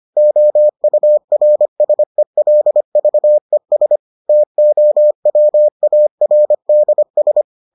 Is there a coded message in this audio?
Yes: Morse code from 0.2 to 7.4 s, heard as a keyed tone, about 25 wpm, 600 Hz. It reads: OURSELVES TOWARDS